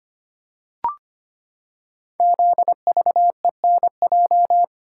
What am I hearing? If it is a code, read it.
Morse code, heard as a keyed tone, about 25 wpm, 700 Hz. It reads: ZVENJ
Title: Code Groups - 5 Letters - 25wpm